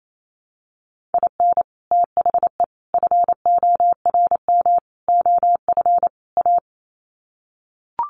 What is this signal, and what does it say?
Morse code, heard as a keyed tone, about 28 wpm, 700 Hz. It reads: IN THE FORM OF A
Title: Sets of 5 Words 28wpm